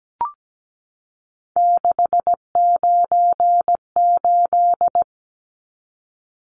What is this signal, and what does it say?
Morse code, heard as a keyed tone, about 17 wpm, 700 Hz. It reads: 698
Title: Numbers - 3-Digit 17wpm